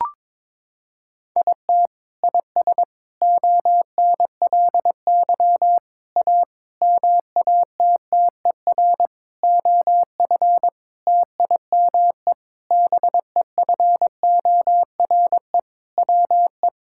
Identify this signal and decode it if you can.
Morse code, heard as a keyed tone, about 22 wpm, 700 Hz. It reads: IT IS ONLY A MATTER OF TIME BEFORE WE
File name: practice / 22wpm / Sentences with 8-Word Phrases - 22wpm